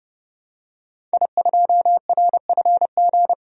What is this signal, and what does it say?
Morse code, heard as a keyed tone, about 30 wpm, 700 Hz. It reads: I2RFG
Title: DXCC-100 Call Signs - 5 Characters - 30wpm